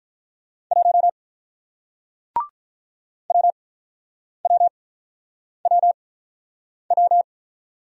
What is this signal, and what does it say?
Morse code, heard as a keyed tone, about 46 wpm, spaced as for 9 wpm, 700 Hz. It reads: 1 WWWW